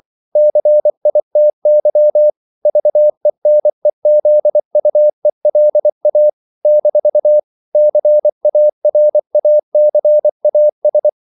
Morse code, heard as a keyed tone, about 24 wpm, 600 Hz. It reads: CITY VENEZUELA - CARACAS
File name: CW / Practice / NationalCapitals24